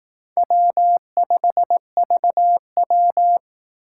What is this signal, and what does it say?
Morse code, heard as a keyed tone, about 18 wpm, 700 Hz. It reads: W5VW
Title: Speed Racing - USA 1x2 Call Signs 15wpm